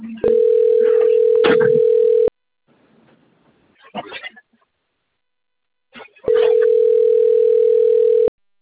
Ringback tone (North America), 440 + 480 Hz.
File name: rg-1600-unknown- - - .90240